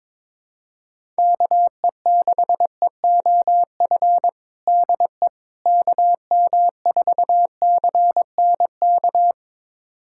Morse code, heard as a keyed tone, about 22 wpm, 700 Hz. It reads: KE6EOF DE KM4CNK
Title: USA Callsign DE Callsign - 6 Characters - 22wpm